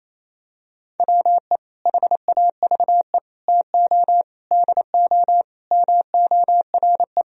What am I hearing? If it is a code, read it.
Morse code, heard as a keyed tone, about 28 wpm, 700 Hz. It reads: WE HAVE TO DO MORE
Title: Sentences from Top 1000 Words - Review Part 3 - 28wpm